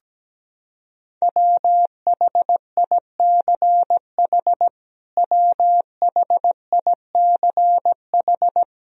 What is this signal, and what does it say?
Morse code, heard as a keyed tone, about 17 wpm, 700 Hz. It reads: WHICH WHICH
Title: Top 100 Words Repeated 17wpm